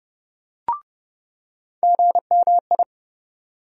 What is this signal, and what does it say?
Morse code, heard as a keyed tone, about 30 wpm, 700 Hz. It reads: GMI
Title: USA Call Sign Postfixes - 3 Letters 30wpm